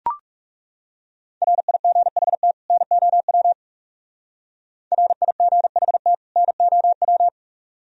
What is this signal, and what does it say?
Morse code, heard as a keyed tone, about 37 wpm, 700 Hz. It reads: RIGHT NOW RIGHT NOW
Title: Speed-Racing - Sets of 2 Words 30wpm